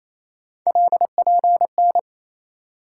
Morse code, heard as a keyed tone, about 28 wpm, 700 Hz. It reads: LPN
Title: Practice QSO - Exchange 3 - More Info - 28wpm